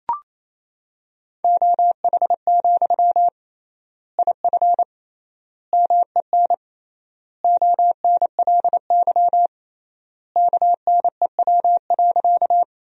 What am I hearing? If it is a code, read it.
Morse code, heard as a keyed tone, about 27 wpm, 700 Hz. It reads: OH, IF MEN ONLY KNEW.
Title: 3x Word-Spacing - Sentences from Top 500 Words - 28wpm